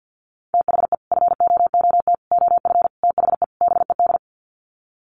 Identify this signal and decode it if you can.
Morse code, heard as a keyed tone, about 50 wpm, 700 Hz. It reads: THE FOOT OF THE BED
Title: Sets of 5 Words Encore2 50wpm